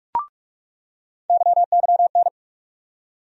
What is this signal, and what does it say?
Morse code, heard as a keyed tone, about 42 wpm, 700 Hz. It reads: YYN
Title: USA Call Sign Postfixes - 3 Letters 45wpm